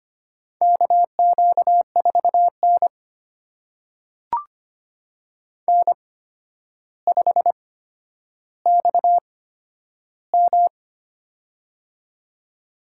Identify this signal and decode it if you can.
Morse code, heard as a keyed tone, about 25 wpm, spaced as for 11 wpm, 700 Hz. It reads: KQ4N N5XM
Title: USA Call Signs - 4 Characters - 8x Character Spacing - 25wpm